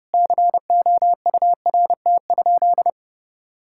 Morse code, heard as a keyed tone, about 30 wpm, 700 Hz. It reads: COURT?
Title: Sentences from Top 1600 Words 30wpm